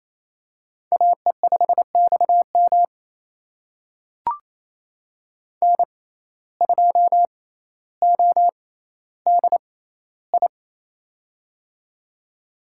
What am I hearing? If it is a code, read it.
Morse code, heard as a keyed tone, about 27 wpm, spaced as for 14 wpm, 700 Hz. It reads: AE5XM N2ODI
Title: USA Call Signs - 5 Characters - 6x Character Spacing - 28wpm